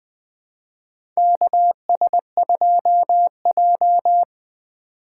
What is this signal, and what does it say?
Morse code, heard as a keyed tone, about 20 wpm, 700 Hz. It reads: KS2J
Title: USA Call Signs - 4 Characters - 4x Character Spacing - 20wpm